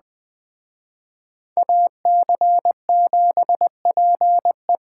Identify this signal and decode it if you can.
Morse code, heard as a keyed tone, about 20 wpm, 700 Hz. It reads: AC7PE
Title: Speed Racing - USA 2x2 Call Signs 17wpm